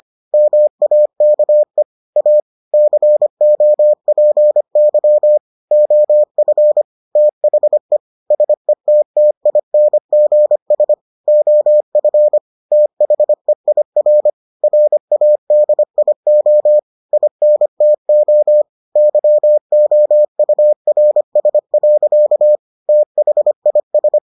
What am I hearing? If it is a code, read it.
Morse code, heard as a keyed tone, about 25 wpm, 600 Hz. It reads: MAKE A COPY OF THE SETTINGS OF THEIR RADIO INTO YOURS. THIS